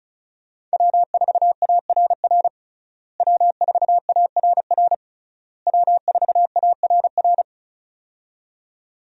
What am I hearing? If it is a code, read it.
Morse code, heard as a keyed tone, about 36 wpm, 700 Hz. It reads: W4ARR W4ARR W4ARR
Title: USA Call Signs - 5 Characters - Repeated 3x - 35wpm